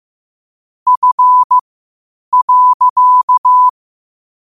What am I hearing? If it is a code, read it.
Morse code, heard as a keyed tone, about 15 wpm, spaced as for 8 wpm, 1000 Hz. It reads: F.
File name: qotdfiles / 8wpm